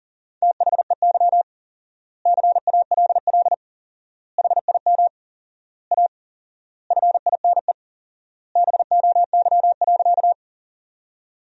Morse code, heard as a keyed tone, about 40 wpm, 700 Hz. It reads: THEY CALL HIM A FINE BOY.
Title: 4x Word-Spacing - Sentences from Top 500 Words - 40wpm